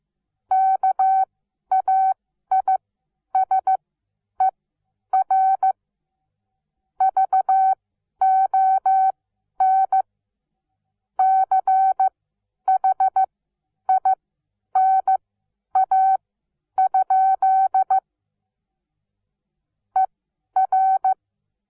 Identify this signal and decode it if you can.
Morse code, heard as a keyed tone, about 15 wpm, spaced as for 10 wpm, 760 Hz. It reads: KAISER VON CHINA? ER
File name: Onlinelehrgang / morsen / DJ4UF Morsen17